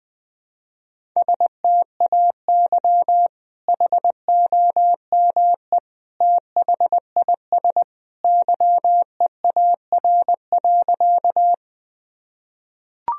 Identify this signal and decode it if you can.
Morse code, heard as a keyed tone, about 20 wpm, 700 Hz. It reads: STAY HOME THIS YEAR.
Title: Sentences from Top 1000 Words - Review Part 1 - 20wpm